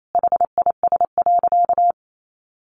Morse code, heard as a keyed tone, about 28 wpm, 700 Hz. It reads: HIS.
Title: Sentences from Top 1000 Words 28wpm